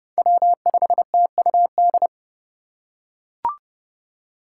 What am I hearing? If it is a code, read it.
Morse code, heard as a keyed tone, about 29 wpm, 700 Hz. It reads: W5TUD